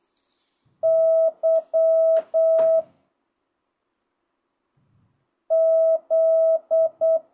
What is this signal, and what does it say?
Morse code, heard as a keyed tone, about 8 wpm, 660 Hz. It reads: Y Z
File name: jh8cbh / 2025cw